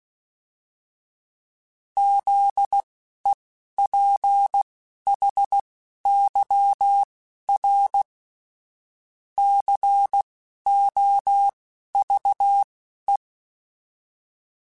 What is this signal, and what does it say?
Morse code, heard as a keyed tone, about 16 wpm, spaced as for 11 wpm, 800 Hz. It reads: ZEPHYR COVE